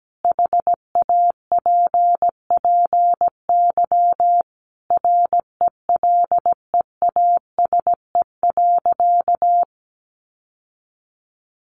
Morse code, heard as a keyed tone, about 17 wpm, 700 Hz. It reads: HAPPY RELEASE.
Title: Sentences from Top 1100 Words 17wpm